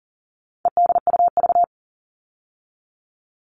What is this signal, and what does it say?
Morse code, heard as a keyed tone, about 40 wpm, 700 Hz. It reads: EDUV
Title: All Cut Numbers - 4-Digit 40wpm